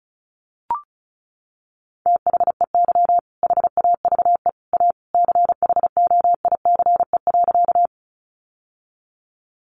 Morse code, heard as a keyed tone, about 34 wpm, 700 Hz. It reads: THEY HAVE A CHOICE.
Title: Sentences from Top 900 Words 35wpm